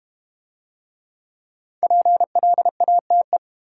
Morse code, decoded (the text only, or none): PLATE